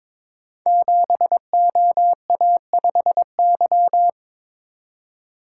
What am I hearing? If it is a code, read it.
Morse code, heard as a keyed tone, about 22 wpm, 690 Hz. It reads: 7OA5Y
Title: Code Groups - 5 Letter:Numbers - 22wpm